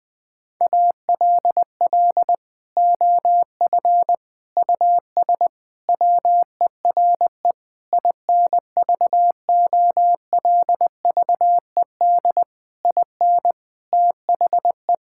Morse code, heard as a keyed tone, about 20 wpm, 700 Hz. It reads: ALL OF US WERE INVOLVED IN THE
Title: Sentences from Top 1000 Words - Review Part 5 - 20wpm